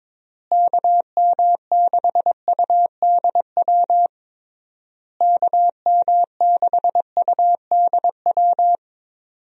Morse code, heard as a keyed tone, about 22 wpm, 700 Hz. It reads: KM6UDW KM6UDW
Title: USA Call Signs - 6 Characters - Repeated 3x - 22wpm